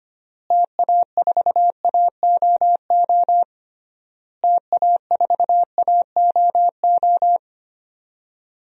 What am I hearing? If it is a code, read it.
Morse code, heard as a keyed tone, about 25 wpm, 700 Hz. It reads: TA4AOO TA4AOO
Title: DXCC-100 Call Signs - 6 Characters - Repeated 3x - 25wpm